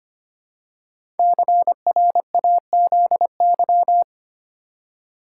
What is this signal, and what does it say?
Morse code, heard as a keyed tone, about 25 wpm, 700 Hz. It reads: CRAZY
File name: 5 Letter Words - First 3 Letters Given - 25wpm